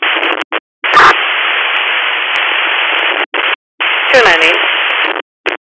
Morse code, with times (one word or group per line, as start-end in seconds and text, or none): none